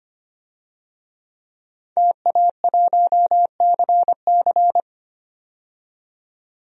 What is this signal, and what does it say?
Morse code, heard as a keyed tone, about 25 wpm, 700 Hz. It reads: TA1CC
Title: DXCC-100 Call Signs - 5 Characters - 25wpm